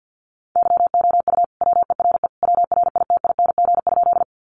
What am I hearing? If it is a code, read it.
Morse code, heard as a keyed tone, about 50 wpm, 700 Hz. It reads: YOU WERE WRITING?
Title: Sentences from Top 800 Words 50wpm